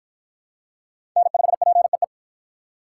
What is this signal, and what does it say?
Morse code, heard as a keyed tone, about 53 wpm, 700 Hz. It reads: N5PEE